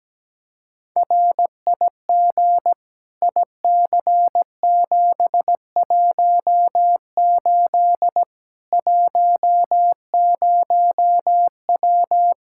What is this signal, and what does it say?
Morse code, heard as a keyed tone, about 17 wpm, 700 Hz. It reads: RIG IC718 10W
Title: Practice QSO - Exchange 2 - Info - 17wpm